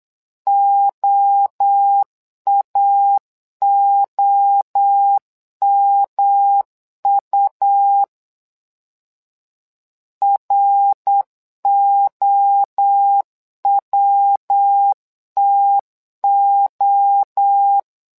Morse code, heard as a keyed tone, about 8 wpm, 800 Hz. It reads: OAOMU ROWTO
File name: LEZIONI / 3 MWRO / 3 MWRO 8wpm 2606